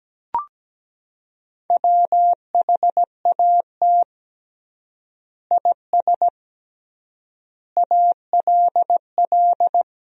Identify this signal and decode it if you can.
Morse code, heard as a keyed tone, about 17 wpm, 700 Hz. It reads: WHAT IS ALL
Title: 3x Word-Spacing - Sentences from Top 100 Words - 17wpm